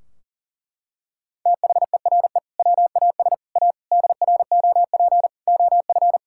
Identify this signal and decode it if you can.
Morse code, heard as a keyed tone, about 40 wpm, 700 Hz. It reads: THERE WAS A DROP OF
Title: Sentences from Top 1600 Words 40wpm